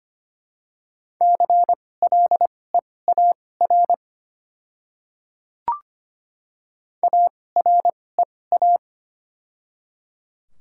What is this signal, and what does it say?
Morse code, heard as a keyed tone, about 25 wpm, spaced as for 18 wpm, 700 Hz. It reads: CLEAR AREA
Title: 2x Character Spacing - Top 500 Words - 25wpm